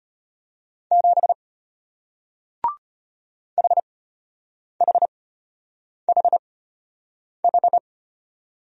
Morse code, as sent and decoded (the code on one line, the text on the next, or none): --... / .... .... .... ....
7 HHHH